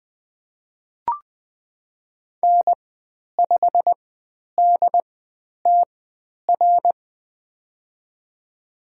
Morse code, heard as a keyed tone, about 20 wpm, spaced as for 10 wpm, 700 Hz. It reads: N5DTR
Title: 102 Lesson 17g - Letter u - Call Signs 20wpm